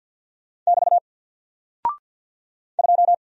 Morse code, heard as a keyed tone, about 50 wpm, 700 Hz. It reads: <BT> 2
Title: Single Character (Letter:Number:Punctuation:Prosign) Alt 50wpm